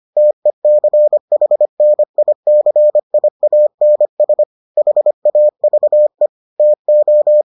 Morse code, heard as a keyed tone, about 25 wpm, 600 Hz. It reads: TECHNICIANS HAVE TO